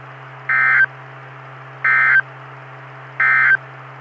Emergency Alert System (SAME) tones, heard as harsh buzzing bursts. It is heard from 0.4 to 3.7 s.
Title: Saline, AK SKYWARN Repeater System - W5RHS